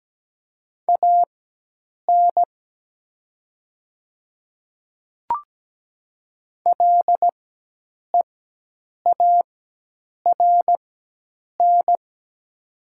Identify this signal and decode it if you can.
Morse code, heard as a keyed tone, about 17 wpm, spaced as for 8 wpm, 700 Hz. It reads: AN LEARN
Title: 4x Character Spacing - Top 300 Words - 17wpm